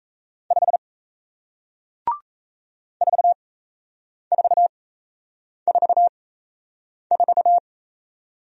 Morse code, heard as a keyed tone, about 36 wpm, spaced as for 8 wpm, 700 Hz. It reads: 5 4444